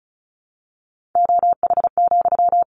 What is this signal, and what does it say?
Morse code, heard as a keyed tone, about 34 wpm, 700 Hz. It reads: OH,
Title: Sentences from Top 900 Words 35wpm